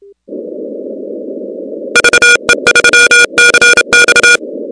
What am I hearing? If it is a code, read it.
Morse code, heard as a keyed tone, about 31 wpm, 1440 Hz, with noise. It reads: VE3CX
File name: contests / 2019-cw-wpx-kt5j-k5pi- -2330 0-16